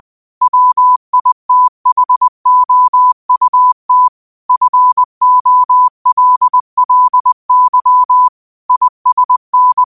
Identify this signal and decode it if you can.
Morse code, heard as a keyed tone, about 20 wpm, 1000 Hz. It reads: WITHOUT FOLLY ISN